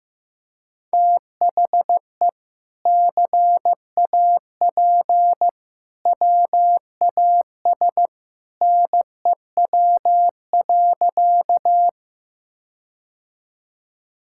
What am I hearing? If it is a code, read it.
Morse code, heard as a keyed tone, about 15 wpm, 700 Hz. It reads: THE CAP WAS NEW.
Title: Sentences from Top 1600 Words 15wpm